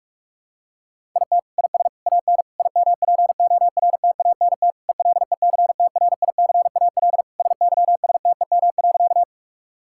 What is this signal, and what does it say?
Morse code, heard as a keyed tone, about 42 wpm, 700 Hz. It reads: IT IS AN IMPORTANT ELECTRICAL SYSTEM.
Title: Sentences from Top 1000 Words - Review Part 6 - 45wpm